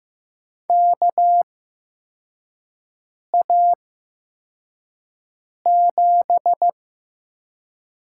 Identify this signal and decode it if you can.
Morse code, heard as a keyed tone, about 15 wpm, spaced as for 4 wpm, 700 Hz. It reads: KA7
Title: USA Call Signs - 6 Characters - 8x Character Spacing - 15wpm